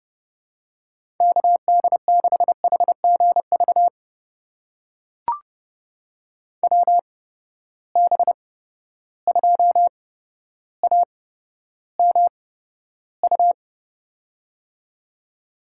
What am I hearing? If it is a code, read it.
Morse code, heard as a keyed tone, about 30 wpm, spaced as for 13 wpm, 700 Hz. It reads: KD6HGV WB2AMU
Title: USA Call Signs - 6 Characters - 8x Character Spacing - 30wpm